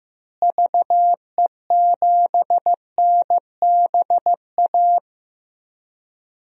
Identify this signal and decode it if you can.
Morse code, heard as a keyed tone, about 15 wpm, 700 Hz. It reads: VE7NBA
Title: DXCC-100 Call Signs - 6 Characters - Repeated 2x - 15wpm